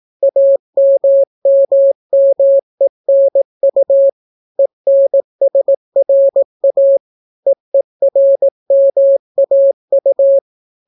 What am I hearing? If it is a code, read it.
Morse code, heard as a keyed tone, about 18 wpm, 550 Hz. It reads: AMMMENU ENSRA EERMAU